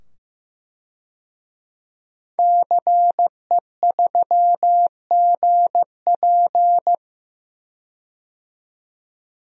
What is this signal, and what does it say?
Morse code, heard as a keyed tone, about 15 wpm, 700 Hz. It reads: CE3GP
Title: DXCC-100 Call Signs - 5 Characters - 15wpm